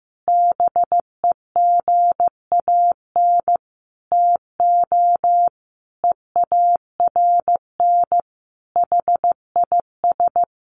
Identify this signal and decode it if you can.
Morse code, heard as a keyed tone, about 15 wpm, 700 Hz. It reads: BEGAN TO EARN HIS
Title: Sentences from Top 1500 Words 15wpm